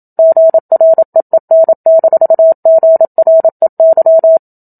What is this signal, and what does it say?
Morse code, heard as a keyed tone, about 27 wpm, 650 Hz. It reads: GREEN-GREY
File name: Book Two / 01-Chapter 1 - 27 WPM